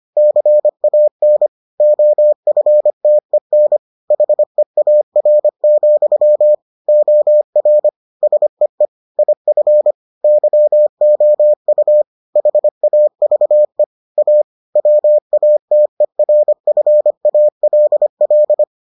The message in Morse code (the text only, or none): CAN OFTEN HEAR, OR SEE IF YOU HAVE A WATERFALL